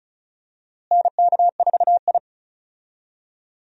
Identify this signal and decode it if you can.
Morse code, heard as a keyed tone, about 36 wpm, 700 Hz. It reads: NK4I
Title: USA Call Signs - 4 Characters - Repeated 2x - 35wpm